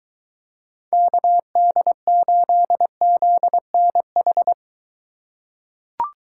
Morse code, heard as a keyed tone, about 23 wpm, 700 Hz. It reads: KD8ZNH